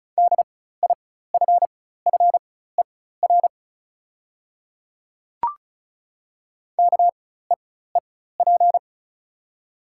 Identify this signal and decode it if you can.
Morse code, heard as a keyed tone, about 36 wpm, spaced as for 16 wpm, 690 Hz. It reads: DIFFER KEEP
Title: 4x Character Spacing - Top 500 Words - 35wpm